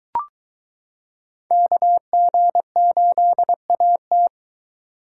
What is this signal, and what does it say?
Morse code, heard as a keyed tone, about 23 wpm, 700 Hz. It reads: KG8AT